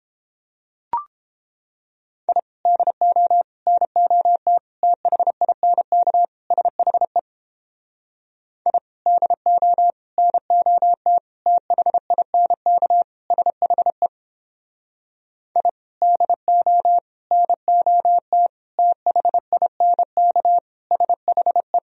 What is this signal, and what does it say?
Morse code, heard as a keyed tone, about 31 wpm, 700 Hz. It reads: I DO NOT THINK SHE I DO NOT THINK SHE I DO NOT THINK SHE